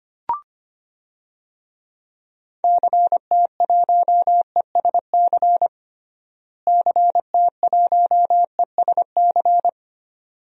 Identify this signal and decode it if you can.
Morse code, heard as a keyed tone, about 25 wpm, 700 Hz. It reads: CT1ESC CT1ESC
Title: DXCC-100 Call Signs - 6 Characters - Repeated 2x - 25wpm